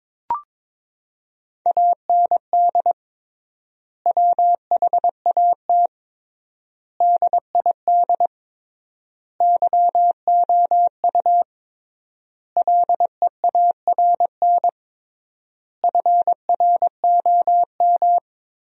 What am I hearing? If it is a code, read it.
Morse code, heard as a keyed tone, about 22 wpm, 700 Hz. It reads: AND WHAT DID YOU LEARN FROM
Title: 3x Word-Spacing - Sentences from Top 300 Words - 22wpm